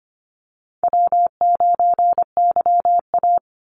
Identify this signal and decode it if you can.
Morse code, heard as a keyed tone, about 25 wpm, 700 Hz. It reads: W9YA
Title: Easy Call Signs Repeated 3x - 25wpm